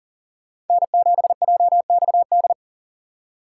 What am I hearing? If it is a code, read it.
Morse code, heard as a keyed tone, about 49 wpm, 700 Hz. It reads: N7JXD
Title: Speed Racing - USA 1x3 Call Signs 40wpm